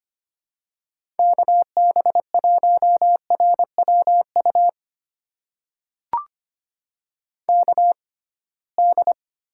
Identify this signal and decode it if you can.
Morse code, heard as a keyed tone, about 25 wpm, spaced as for 19 wpm, 700 Hz. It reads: KB1RWU KD
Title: USA Call Signs - 6 Characters - 6x Character Spacing - 25wpm